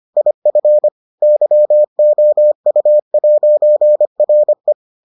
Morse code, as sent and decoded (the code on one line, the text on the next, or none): .. ..-. / -.-- --- ..- .----. .-. .
IF YOU'RE